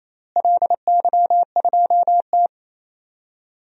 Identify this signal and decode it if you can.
Morse code, heard as a keyed tone, about 28 wpm, 700 Hz. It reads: LY2T
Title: DXCC-100 Call Signs - 4 Characters - Repeated 2x - 28wpm